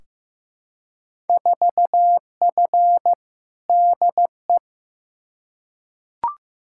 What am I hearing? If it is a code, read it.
Morse code, heard as a keyed tone, about 15 wpm, 700 Hz. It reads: 4F DE
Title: Field Day Exchanges Encore - 15wpm